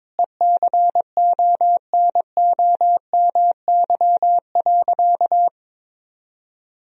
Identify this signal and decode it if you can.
Morse code, heard as a keyed tone, about 22 wpm, 700 Hz. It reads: ECONOMY.
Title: Sentences from Top 1000 Words - Review Part 3 - 22wpm